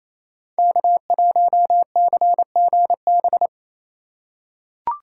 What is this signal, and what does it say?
Morse code, heard as a keyed tone, about 28 wpm, 700 Hz. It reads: K1CGB
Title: Code Groups - 5 Letter:Numbers - 28wpm